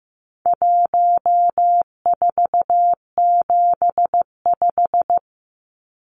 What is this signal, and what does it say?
Morse code, heard as a keyed tone, about 15 wpm, 700 Hz. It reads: 1475
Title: Numbers - 4-Digit 15wpm